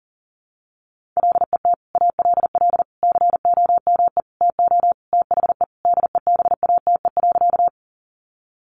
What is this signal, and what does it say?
Morse code, heard as a keyed tone, about 40 wpm, 700 Hz. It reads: LET ALL COME TO THE DEBATE.
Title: Sentences from Top 1100 Words 40wpm